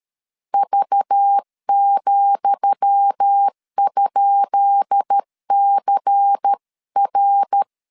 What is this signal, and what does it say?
Morse code, heard as a keyed tone, about 13 wpm, 800 Hz. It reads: V,?CR
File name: LEZIONI / 10 SIGNS / 10D COMA 12wpm 2544